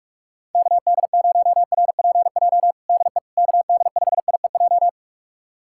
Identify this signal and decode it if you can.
Morse code, heard as a keyed tone, about 46 wpm, 690 Hz. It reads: KD0RPJ DE KD5IEJ
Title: USA Callsign DE Callsign - 6 Characters - 45wpm